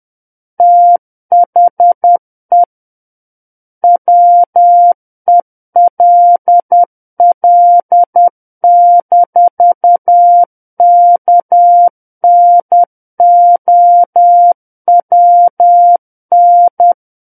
Morse code, heard as a keyed tone, about 10 wpm, 700 Hz. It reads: THE WELL-KNOWN
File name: Book One / 01-Chapter 1 - 10 WPM